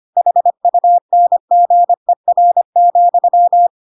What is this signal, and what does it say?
Morse code, heard as a keyed tone, about 25 wpm, 700 Hz. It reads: HUNGER,
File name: Pinocchio / Pinocchio ch5 25 25 ik1hge